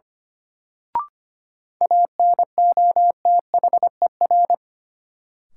Morse code, heard as a keyed tone, about 25 wpm, 700 Hz. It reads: ANOTHER
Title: Sentences from Top 600 Words - New Word Drill - 25wpm